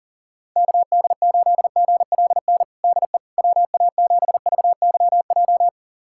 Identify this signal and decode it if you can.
Morse code, heard as a keyed tone, about 37 wpm, 700 Hz. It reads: KD8GLN DE WA7VYJ
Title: USA Callsign DE Callsign 40wpm